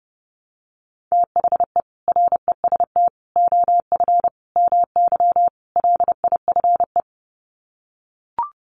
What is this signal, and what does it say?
Morse code, heard as a keyed tone, about 30 wpm, 700 Hz. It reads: THE REST OF MY LIFE
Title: Sets of 5 Words 30wpm